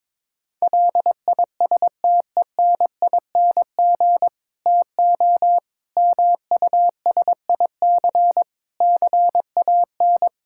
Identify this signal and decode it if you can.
Morse code, heard as a keyed tone, about 22 wpm, 700 Hz. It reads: LISTENING TO MUSIC CAN
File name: Difficult Sentences with Hints - 22wpm